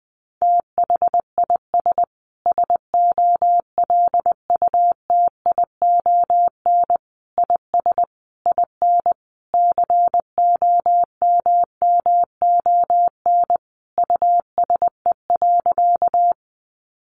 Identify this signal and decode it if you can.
Morse code, heard as a keyed tone, about 20 wpm, 700 Hz. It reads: THIS SOLUTION IS IN COMMON USE.
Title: Sentences from Top 1300 Words 20wpm